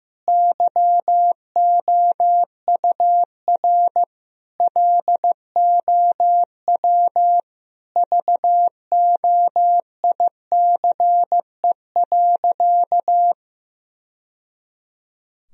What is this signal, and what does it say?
Morse code, heard as a keyed tone, about 15 wpm, 700 Hz. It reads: YOUR LOW VOICE.
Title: Sentences from Top 1000 Words - Review Part 1 - 15wpm